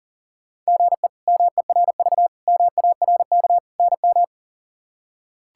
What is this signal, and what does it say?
Morse code, heard as a keyed tone, about 40 wpm, 700 Hz. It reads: GE MERV MARK NM
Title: K1USN SST QSO Practice - 40wpm